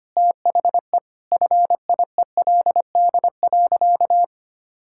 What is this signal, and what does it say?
Morse code, heard as a keyed tone, about 25 wpm, 700 Hz. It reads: THE FIELD.
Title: Sentences from Top 1000 Words - Review Part 5 - 25wpm